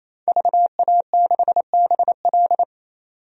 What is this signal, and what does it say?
Morse code, heard as a keyed tone, about 28 wpm, 700 Hz. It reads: VA6BL
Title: DXCC-100 Call Signs - 5 Characters - 28wpm